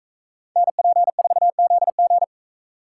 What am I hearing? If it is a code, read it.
Morse code, heard as a keyed tone, about 41 wpm, 700 Hz. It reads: NP4ZG